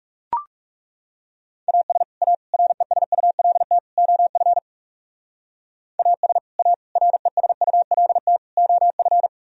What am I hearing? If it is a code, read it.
Morse code, heard as a keyed tone, about 37 wpm, 700 Hz. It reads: AS A RESULT OF AS A RESULT OF
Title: Speed-Racing - Sets of 4 Words 30wpm